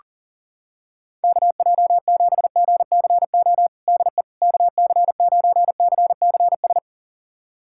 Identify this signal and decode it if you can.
Morse code, heard as a keyed tone, about 40 wpm, 700 Hz. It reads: KJ7GCO DE KC9CCS
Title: USA Callsign DE Callsign 40wpm